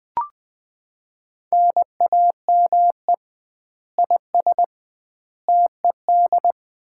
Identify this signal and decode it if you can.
Morse code, heard as a keyed tone, about 20 wpm, 700 Hz. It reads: NAME IS TED
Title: Common QSO Phrases - 2x Word-Spacing - 20wpm